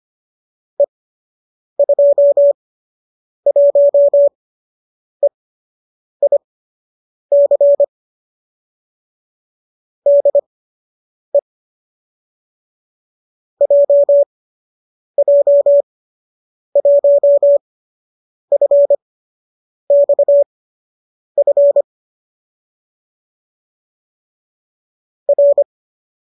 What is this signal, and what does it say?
Morse code, heard as a keyed tone, about 25 wpm, spaced as for 8 wpm, 570 Hz. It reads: E21EIC DE JJ1FXF R